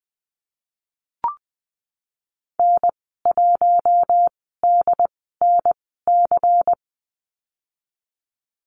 Morse code, heard as a keyed tone, about 19 wpm, spaced as for 14 wpm, 700 Hz. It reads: N1DNC
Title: 173 Lesson 28g - Number 9 - Call Signs 20wpm